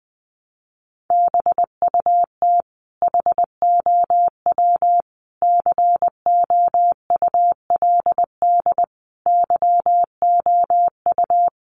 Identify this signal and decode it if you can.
Morse code, heard as a keyed tone, about 20 wpm, 700 Hz. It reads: BUT HOW COULD YOU
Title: Sentences from Top 1200 Words 20wpm